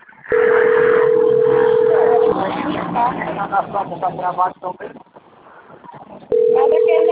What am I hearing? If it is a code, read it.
Ringback tone (North America), 440 + 480 Hz, with speech.
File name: rg-1600-unknown- - - .15753